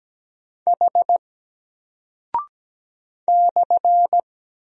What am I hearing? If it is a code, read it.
Morse code, heard as a keyed tone, about 17 wpm, 700 Hz. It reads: H /
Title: Single Character (Letter:Number:Punctuation:Prosign) 17wpm